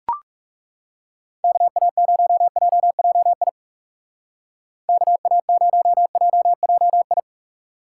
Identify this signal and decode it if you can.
Morse code, heard as a keyed tone, about 36 wpm, 700 Hz. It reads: KA0JJI KA0JJI
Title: Speed Racing - USA Call Signs 30wpm